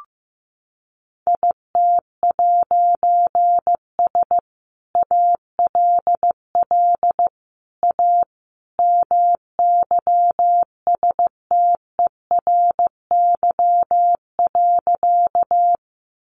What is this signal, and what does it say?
Morse code, heard as a keyed tone, about 15 wpm, 700 Hz. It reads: IT'S ALL A MYSTERY.
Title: A Sound Of Thunder 15wpm